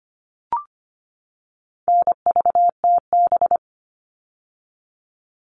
Morse code, heard as a keyed tone, about 25 wpm, 700 Hz. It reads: NVTB
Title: All Cut Numbers - 4-Digit 25wpm